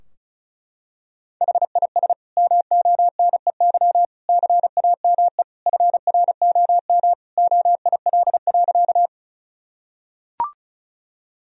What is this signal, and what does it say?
Morse code, heard as a keyed tone, about 34 wpm, 700 Hz. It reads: HIS MONEY CAME FROM OIL.
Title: Sentences from Top 1000 Words - Review Part 4 - 35wpm